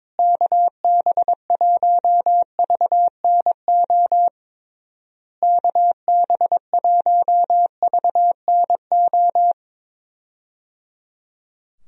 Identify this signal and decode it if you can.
Morse code, heard as a keyed tone, about 22 wpm, 700 Hz. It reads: KB1VNO KB1VNO
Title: USA Call Signs - 6 Characters - Repeated 3x - 22wpm